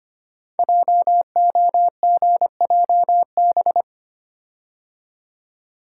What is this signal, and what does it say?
Morse code, heard as a keyed tone, about 26 wpm, 700 Hz. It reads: JOGJB
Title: Code Groups - 5 Letters - 25wpm